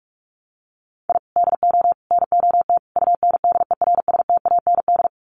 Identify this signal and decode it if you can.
Morse code, heard as a keyed tone, about 44 wpm, 700 Hz. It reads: I DO NOT UNDERSTAND
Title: Sets of 4 Words 45wpm